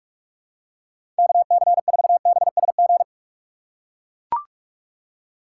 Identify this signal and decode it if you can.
Morse code, heard as a keyed tone, about 41 wpm, 700 Hz. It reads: KC4BSG
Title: USA Call Signs - 6 Characters - 4x Character Spacing - 45wpm